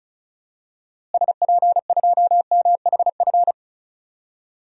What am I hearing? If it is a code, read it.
Morse code, heard as a keyed tone, about 36 wpm, 700 Hz. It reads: SP2MHF
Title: DXCC-100 Call Signs - 6 Characters - Repeated 2x - 35wpm